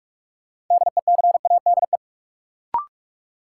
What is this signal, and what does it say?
Morse code, heard as a keyed tone, about 46 wpm, 700 Hz. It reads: DECADE
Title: Sets of 5 Words - New Word Drill - 45wpm